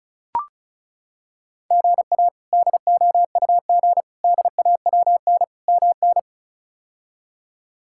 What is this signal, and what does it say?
Morse code, heard as a keyed tone, about 36 wpm, 700 Hz. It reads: GA DOUG DAWN MN
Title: K1USN SST QSO Practice - 35wpm